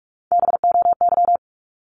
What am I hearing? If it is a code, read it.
Morse code, heard as a keyed tone, about 44 wpm, 700 Hz. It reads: BOY
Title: Top 500 Words Repeated 45wpm